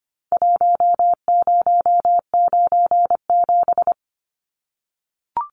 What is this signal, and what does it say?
Morse code, heard as a keyed tone, about 25 wpm, 700 Hz. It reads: 1097
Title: Numbers - Contest Style - 4-Digit 25wpm